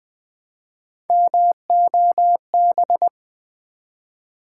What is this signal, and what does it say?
Morse code, heard as a keyed tone, about 20 wpm, 700 Hz. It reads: MOB
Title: Build A Word - 1 Letter At A Time - 3-Letter Words - 20wpm